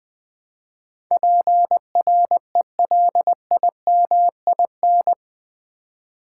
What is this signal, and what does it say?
Morse code, heard as a keyed tone, about 20 wpm, 700 Hz. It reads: PRELIMIN
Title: Build A Word - 4 Letters At A Time - 11-Letter Words - 20wpm